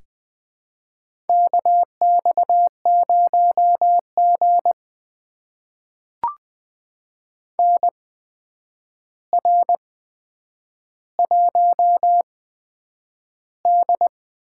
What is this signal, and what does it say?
Morse code, heard as a keyed tone, about 20 wpm, spaced as for 9 wpm, 700 Hz. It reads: KX0G NR1D
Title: USA Call Signs - 4 Characters - 8x Character Spacing - 20wpm